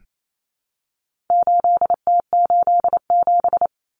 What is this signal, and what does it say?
Morse code, heard as a keyed tone, about 28 wpm, 700 Hz. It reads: 8T87
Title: Common Cut Numbers - 4-Digit 28wpm